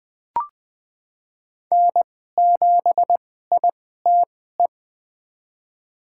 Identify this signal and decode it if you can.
Morse code, heard as a keyed tone, about 20 wpm, spaced as for 14 wpm, 700 Hz. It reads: N7ITE
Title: USA Call Signs - 5 Characters - 2x Character Spacing - 20wpm